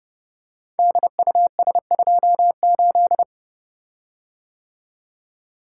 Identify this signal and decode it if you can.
Morse code, heard as a keyed tone, about 30 wpm, 700 Hz. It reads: DUS28
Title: Code Groups - 5 Letter:Numbers - 30wpm